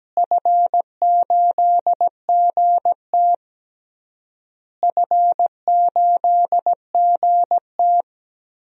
Morse code, heard as a keyed tone, about 17 wpm, 700 Hz. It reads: F8GT F8GT
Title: DXCC-100 Call Signs - 4 Characters - Repeated 3x - 17wpm